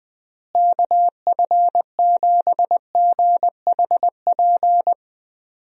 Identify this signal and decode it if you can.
Morse code, heard as a keyed tone, about 20 wpm, 700 Hz. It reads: KF7GHP
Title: USA Call Signs - 6 Characters - Repeated 2x - 20wpm